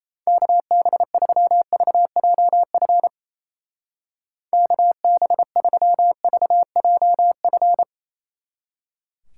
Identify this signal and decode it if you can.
Morse code, heard as a keyed tone, about 35 wpm, 700 Hz. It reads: KB3VJF KB3VJF
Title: Speed Racing - USA 2x3 Call Signs 28wpm